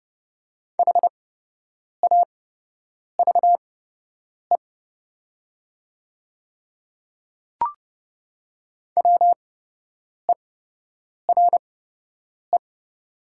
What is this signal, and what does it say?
Morse code, heard as a keyed tone, about 30 wpm, spaced as for 8 wpm, 700 Hz. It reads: HAVE WERE
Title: 8x Character Spacing - Top 200 Words - 30wpm